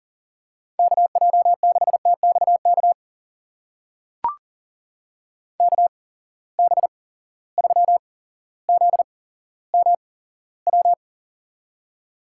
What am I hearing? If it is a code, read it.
Morse code, heard as a keyed tone, about 40 wpm, spaced as for 17 wpm, 700 Hz. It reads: KJ6TXK KB3ZMW
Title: USA Call Signs - 6 Characters - 8x Character Spacing - 40wpm